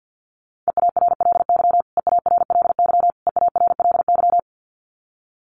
Morse code, heard as a keyed tone, about 50 wpm, 700 Hz. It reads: EARLY EARLY EARLY
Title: Top 500 Words Repeated 50wpm